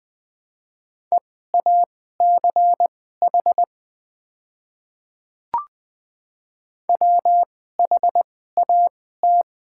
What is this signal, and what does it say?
Morse code, heard as a keyed tone, about 20 wpm, spaced as for 14 wpm, 700 Hz. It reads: EACH WHAT
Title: 2x Character Spacing - Top 100 Words - 20wpm